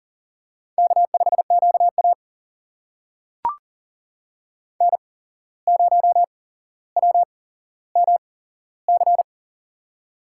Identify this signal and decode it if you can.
Morse code, heard as a keyed tone, about 40 wpm, spaced as for 16 wpm, 700 Hz. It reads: K5QA N0WMC
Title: USA Call Signs - 8x Character Spacing - 40wpm